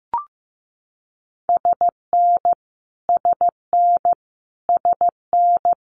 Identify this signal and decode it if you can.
Morse code, heard as a keyed tone, about 15 wpm, 700 Hz. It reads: SN SN SN
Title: Multi-letter Phonemes Repeated 3x 15wpm